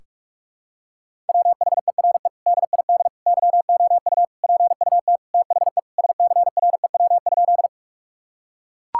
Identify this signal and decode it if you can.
Morse code, heard as a keyed tone, about 41 wpm, 700 Hz. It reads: WHERE DID YOU PUT THE SCREW?
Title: Sentences from Top 1600 Words 45wpm